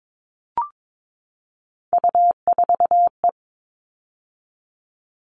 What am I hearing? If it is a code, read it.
Morse code, heard as a keyed tone, about 22 wpm, 700 Hz. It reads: U4E
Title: All Cut Numbers - 3-Digit 22wpm